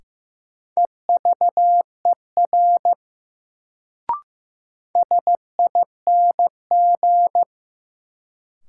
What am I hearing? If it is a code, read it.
Morse code, heard as a keyed tone, about 15 wpm, 700 Hz. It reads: EVER SING
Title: Sentences from Top 400 Words - New Word Drill - 15wpm